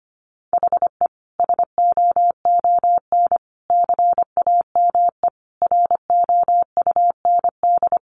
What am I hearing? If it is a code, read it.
Morse code, heard as a keyed tone, about 25 wpm, 700 Hz. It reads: HE SOON CAME ROUND
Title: Sentences from Top 600 Words 25wpm